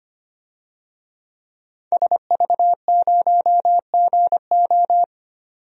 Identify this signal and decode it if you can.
Morse code, heard as a keyed tone, about 25 wpm, 700 Hz. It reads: SV0GO
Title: DXCC-100 Call Signs - 5 Characters - 25wpm